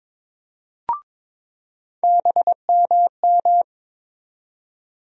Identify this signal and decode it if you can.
Morse code, heard as a keyed tone, about 22 wpm, 700 Hz. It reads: BMM
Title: USA Call Sign Postfixes - 3 Letters 22wpm